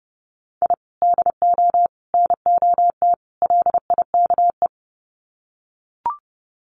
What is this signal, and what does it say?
Morse code, heard as a keyed tone, about 30 wpm, 700 Hz. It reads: I DO NOT LIKE
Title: Sets of 4 Words 30wpm